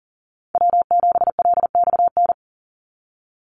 Morse code, heard as a keyed tone, about 40 wpm, 700 Hz. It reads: W7LXN
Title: Call Signs Repeated 3x - 40wpm